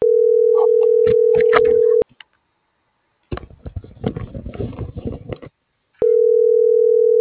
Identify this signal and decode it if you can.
Ringback tone (North America), 440 + 480 Hz.